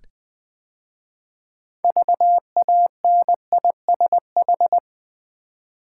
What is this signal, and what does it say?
Morse code, heard as a keyed tone, about 20 wpm, 700 Hz. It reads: VANISH
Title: 6 Letter Words - First 3 Letters Given - 20wpm